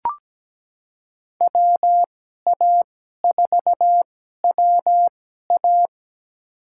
Morse code, heard as a keyed tone, about 17 wpm, spaced as for 12 wpm, 700 Hz. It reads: WA4WA
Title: USA Call Signs - 5 Characters - 2x Character Spacing - 17wpm